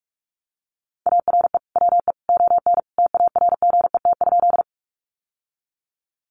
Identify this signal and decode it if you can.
Morse code, heard as a keyed tone, about 44 wpm, 700 Hz. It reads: ARE WE ON TARGET?
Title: Sentences from Top 1200 Words 45wpm